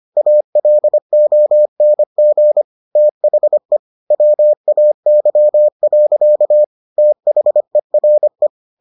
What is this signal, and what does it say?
Morse code, heard as a keyed tone, about 25 wpm, 600 Hz. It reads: ALONG THE WAY. THERE